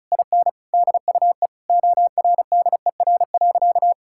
Morse code, heard as a keyed tone, about 36 wpm, 700 Hz. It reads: IN DUE ORDER.
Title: Sentences from Top 1000 Words - Review Part 6 - 35wpm